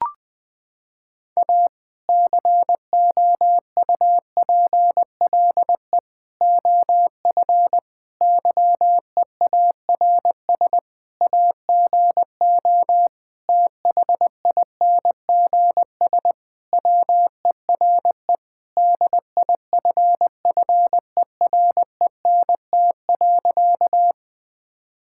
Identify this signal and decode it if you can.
Morse code, heard as a keyed tone, about 20 wpm, 700 Hz. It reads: A COUPLE OF YEARS AGO THINGS WERE DIFFERENT.
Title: Sentences with 5-Word Phrases - 20wpm